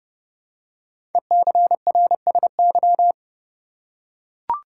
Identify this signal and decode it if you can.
Morse code, heard as a keyed tone, about 30 wpm, 700 Hz. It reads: ECRSY
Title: Code Groups - 5 Letters - 30wpm